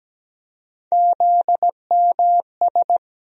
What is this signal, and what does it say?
Morse code, heard as a keyed tone, about 17 wpm, 700 Hz. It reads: ZMS
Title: USA Call Sign Postfixes - 3 Letters 17wpm